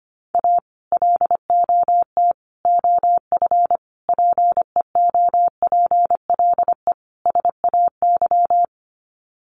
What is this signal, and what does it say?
Morse code, heard as a keyed tone, about 25 wpm, 700 Hz. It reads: A LOT OF PEOPLE SAY
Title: Sets of 5 Words Encore2 25wpm